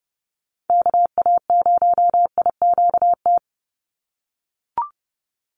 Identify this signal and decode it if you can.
Morse code, heard as a keyed tone, about 30 wpm, 700 Hz. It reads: KA0IQT
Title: Call Signs - 30wpm